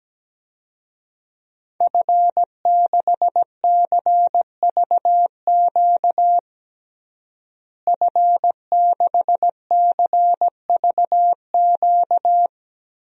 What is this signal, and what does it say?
Morse code, heard as a keyed tone, about 17 wpm, 700 Hz. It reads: F6CVQ F6CVQ
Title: DXCC-100 Call Signs - 5 Characters - Repeated 3x - 17wpm